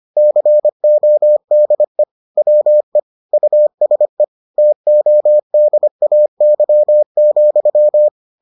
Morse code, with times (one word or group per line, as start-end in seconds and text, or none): 0.2-2.0 CODE
2.4-3.0 WE
3.3-4.2 USE
4.6-8.1 TODAY,